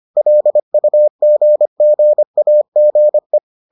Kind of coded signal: Morse code

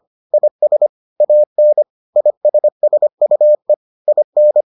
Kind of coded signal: Morse code

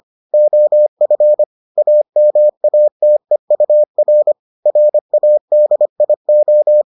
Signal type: Morse code

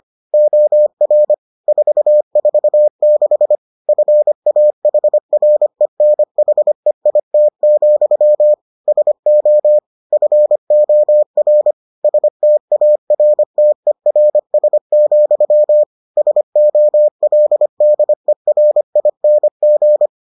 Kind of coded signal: Morse code

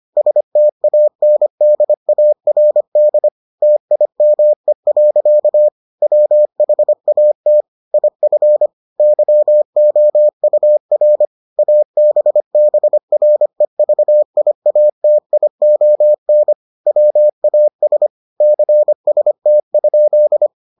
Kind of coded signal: Morse code

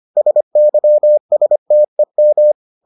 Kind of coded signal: Morse code

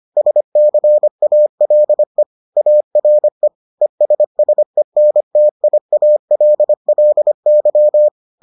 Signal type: Morse code

